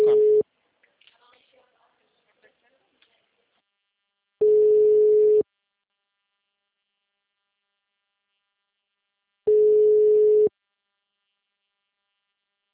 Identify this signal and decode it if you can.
Ringback tone (Europe, 425 Hz), with speech.